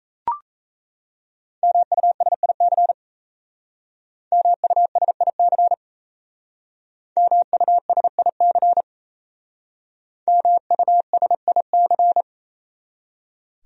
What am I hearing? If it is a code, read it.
Morse code, heard as a keyed tone, about 36 wpm, 700 Hz. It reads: MUSIC MUSIC MUSIC MUSIC